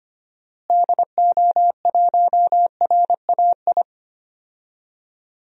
Morse code, heard as a keyed tone, about 25 wpm, 700 Hz. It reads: DO1RAI
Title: DXCC-100 Call Signs - 6 Characters - Repeated 2x - 25wpm